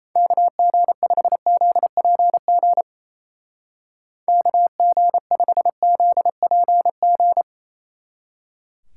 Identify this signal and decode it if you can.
Morse code, heard as a keyed tone, about 30 wpm, 700 Hz. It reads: KG5ZPG KG5ZPG
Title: Speed Racing - USA 2x3 Call Signs 28wpm